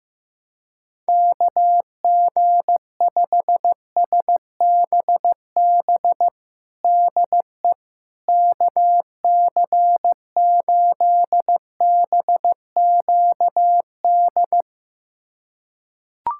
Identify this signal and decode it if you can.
Morse code, heard as a keyed tone, about 16 wpm, 700 Hz. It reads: KG5SBB DE KC8BQD
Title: USA Callsign DE Callsign - 6 Characters - 15wpm